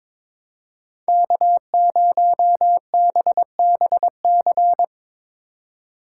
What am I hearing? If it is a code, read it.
Morse code, heard as a keyed tone, about 22 wpm, 700 Hz. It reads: K0BBC
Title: USA Call Signs - 5 Characters - 2x Character Spacing - 22wpm